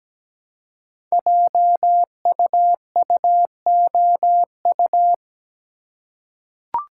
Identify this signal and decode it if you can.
Morse code, heard as a keyed tone, about 17 wpm, 700 Hz. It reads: JUUOU
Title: Code Groups - 5 Letters - 17wpm